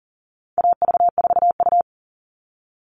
Morse code, heard as a keyed tone, about 40 wpm, 700 Hz. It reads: AV4U
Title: All Cut Numbers - 4-Digit 40wpm